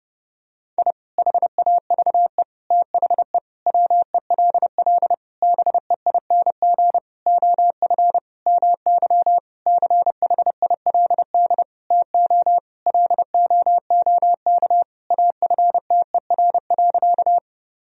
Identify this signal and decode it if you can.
Morse code, heard as a keyed tone, about 30 wpm, 700 Hz. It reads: I HAVE THE WELL BEING OF MY CHILD TO LOOK AFTER.
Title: Sentences from Top 1000 Words - Review Part 3 - 30wpm